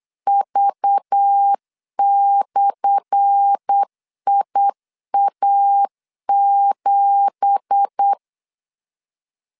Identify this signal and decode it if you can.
Morse code, heard as a keyed tone, about 8 wpm, 800 Hz. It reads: V/IA7
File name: LEZIONI / 10 SIGNS / 10A SLASH 8wpm 3452